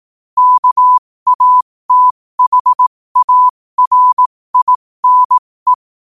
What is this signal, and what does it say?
Morse code, heard as a keyed tone, about 18 wpm, spaced as for 15 wpm, 1000 Hz. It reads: KATHARINE
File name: qotdfiles / 15wpm